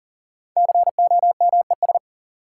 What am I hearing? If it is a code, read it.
Morse code, heard as a keyed tone, about 39 wpm, 700 Hz. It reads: COMES
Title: Sets of 3 Words - New Word Drill - 40wpm